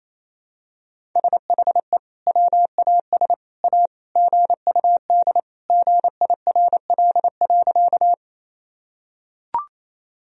Morse code, heard as a keyed tone, about 28 wpm, 700 Hz. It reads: SHE WAS A GUD GIRL.
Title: Sentences from Top 400 Words and Abbreviations 28wpm